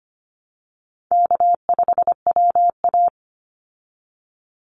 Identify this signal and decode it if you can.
Morse code, heard as a keyed tone, about 25 wpm, 700 Hz. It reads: K5WA
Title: Call Signs Repeated 2x - 25wpm